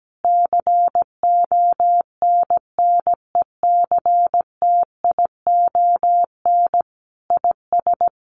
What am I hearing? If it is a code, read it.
Morse code, heard as a keyed tone, about 17 wpm, 700 Hz. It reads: CONNECTION IS
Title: Sentences from Top 1300 Words 17wpm